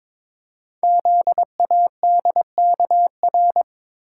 Morse code, heard as a keyed tone, about 23 wpm, 700 Hz. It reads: ZADKR
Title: Code Groups - 5 Letters - 22wpm